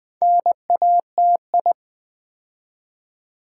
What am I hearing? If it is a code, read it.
Morse code, heard as a keyed tone, about 20 wpm, 700 Hz. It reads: NATI
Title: Build A Word - 2 Letters At A Time - 8-Letter Words - 20wpm